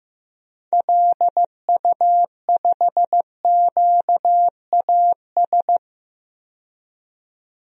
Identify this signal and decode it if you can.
Morse code, heard as a keyed tone, about 15 wpm, 700 Hz. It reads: LU5QAS
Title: DXCC-100 Call Signs - 6 Characters - 15wpm